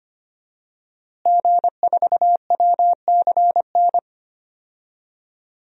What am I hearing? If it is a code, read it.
Morse code, heard as a keyed tone, about 25 wpm, 700 Hz. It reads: G4WCN
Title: DXCC-100 Call Signs - 5 Characters - Repeated 2x - 25wpm